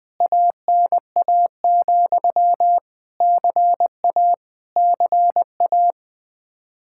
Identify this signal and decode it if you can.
Morse code, heard as a keyed tone, about 20 wpm, 700 Hz. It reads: ANA, CA CA
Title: Practice QSO - Exchange 1 - Beginning - 20wpm